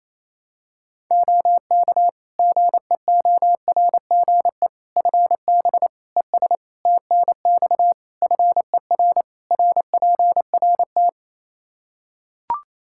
Morse code, heard as a keyed tone, about 28 wpm, 700 Hz. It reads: OK GEORGE FB ES TNX FER RPRT
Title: K1USN SST QSO Practice 28wpm